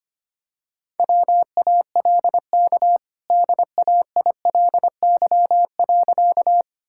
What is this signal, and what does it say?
Morse code, heard as a keyed tone, about 25 wpm, 700 Hz. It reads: WALK DAILY.
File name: Easy Sentences with Hints - 25wpm